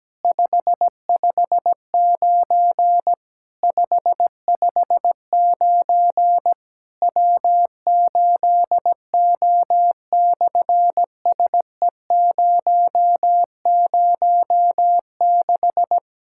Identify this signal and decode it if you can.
Morse code, heard as a keyed tone, about 17 wpm, 700 Hz. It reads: 559 559 W8O/SE006
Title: SOTA QSO Practice 17wpm